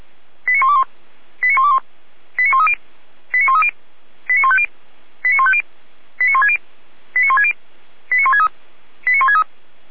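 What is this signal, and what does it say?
Five-tone selective call (ZVEI), heard as quick runs of five tones, tones of about 70 ms. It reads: 89121 89121 89130 89130 89150 89150 89160 89160 89163 89163
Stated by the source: ZVEI five-tone call received by SDR, archived at audio.f85360.feuerwehr-orlamuende.org (fire brigade Orlamünde).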